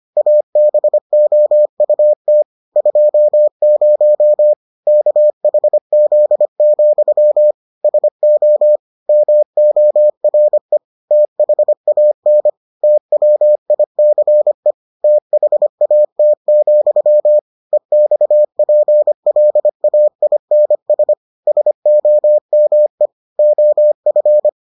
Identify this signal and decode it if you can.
Morse code, heard as a keyed tone, about 25 wpm, 600 Hz. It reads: ABOUT 20 KHZ, SO MORE THAN TWICE THAT, EXPLAINS SOME OF